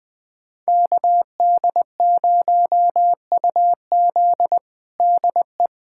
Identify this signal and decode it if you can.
Morse code, heard as a keyed tone, about 21 wpm, 700 Hz. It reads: KD0UZ DE
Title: USA Callsign DE Callsign - 5 Characters - 20wpm